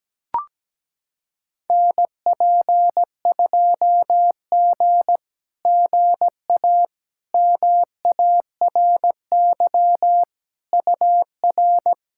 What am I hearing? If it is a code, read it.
Morse code, heard as a keyed tone, about 17 wpm, 700 Hz. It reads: NP2G GA MARY UR
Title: SOTA QSO Practice 17wpm